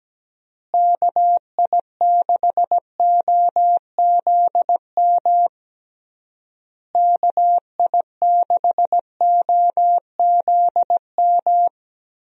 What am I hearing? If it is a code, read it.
Morse code, heard as a keyed tone, about 17 wpm, 700 Hz. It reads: KI6OZM KI6OZM
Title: USA Call Signs - 6 Characters - Repeated 3x - 17wpm